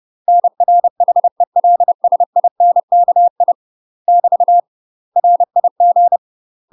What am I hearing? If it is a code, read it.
Morse code, heard as a keyed tone, about 30 wpm, 700 Hz. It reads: NR HELSINKI <BT> RIG
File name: UK MASTER QSOs TRAINING / 30WPM / UK MASTER QSOs 0016